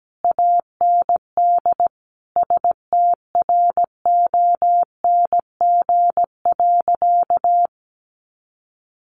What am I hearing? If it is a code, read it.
Morse code, heard as a keyed tone, about 17 wpm, 700 Hz. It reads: AND STRONG.
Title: Sentences from Top 1400 Words 17wpm